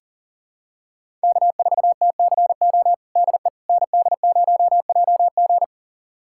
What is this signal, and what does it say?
Morse code, heard as a keyed tone, about 40 wpm, 700 Hz. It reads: K4TCO DE ND0JG
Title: USA Callsign DE Callsign - 5 Characters - 40wpm